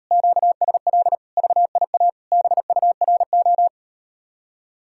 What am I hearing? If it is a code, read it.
Morse code, heard as a keyed tone, about 38 wpm, 700 Hz. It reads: QSL VIA BURO